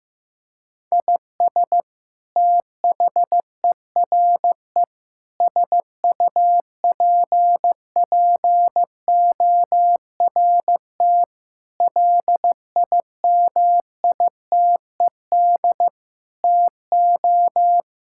Morse code, heard as a keyed tone, about 15 wpm, 700 Hz. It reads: IS THERE SUPPORT LIMITED TO
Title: Sentences from Top 1000 Words - Review Part 6 - 15wpm